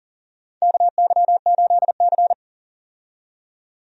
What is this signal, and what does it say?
Morse code, heard as a keyed tone, about 40 wpm, 700 Hz. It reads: KY8C
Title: USA Call Signs - 4 Characters - 40wpm